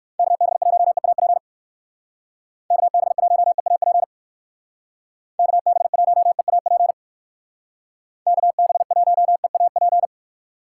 Morse code, heard as a keyed tone, about 54 wpm, 700 Hz. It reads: KB1EAP KB1EAP KB1EAP KB1EAP
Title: Speed Racing - USA 2x3 Call Signs 45wpm